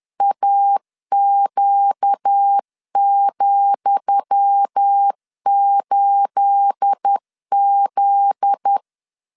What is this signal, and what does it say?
Morse code, heard as a keyed tone, about 10 wpm, 800 Hz. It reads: AQ,8Z
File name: LEZIONI / 10 SIGNS / 10D COMA 10wpm 3013